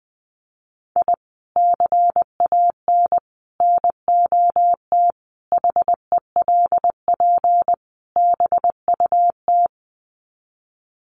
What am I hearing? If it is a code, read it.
Morse code, heard as a keyed tone, about 19 wpm, 700 Hz. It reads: I CAN NOT HELP BUT
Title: Sets of 5 Words Encore2 20wpm